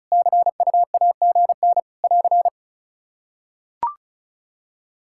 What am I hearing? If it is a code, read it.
Morse code, heard as a keyed tone, about 34 wpm, 700 Hz. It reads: CUAGN <AR>
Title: K1USN SST QSO Practice 35wpm